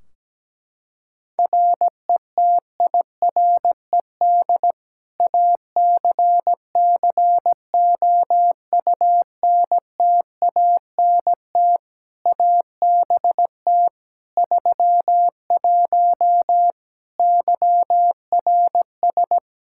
Morse code, heard as a keyed tone, about 17 wpm, 700 Hz. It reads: RETIRED ACCOUNTANT ABT 31 YRS
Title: K1USN SST QSO Practice 17wpm